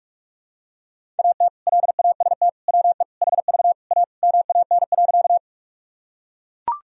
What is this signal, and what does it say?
Morse code, heard as a keyed tone, about 41 wpm, 700 Hz. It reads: AT LAST WE HV A MAN.
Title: Sentences from Top 300 Words and Abbreviations 45wpm